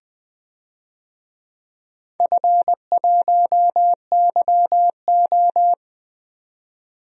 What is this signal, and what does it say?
Morse code, heard as a keyed tone, about 20 wpm, 700 Hz. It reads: F1YO
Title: DXCC-100 Call Signs - 4 Characters - 20wpm